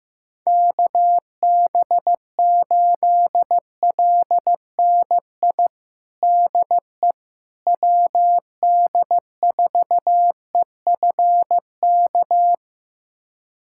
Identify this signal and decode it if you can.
Morse code, heard as a keyed tone, about 15 wpm, 700 Hz. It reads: KB8LNI DE WD4EFK
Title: USA Callsign DE Callsign 15wpm